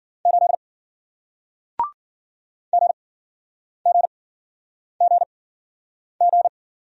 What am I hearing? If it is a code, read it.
Morse code, heard as a keyed tone, about 51 wpm, spaced as for 9 wpm, 700 Hz. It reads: 8 GGGG